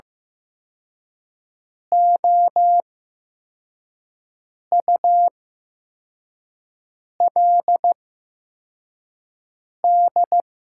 Morse code, heard as a keyed tone, about 15 wpm, spaced as for 4 wpm, 700 Hz. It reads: OULD
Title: 8x Character Spacing - Top 300 Words - 15wpm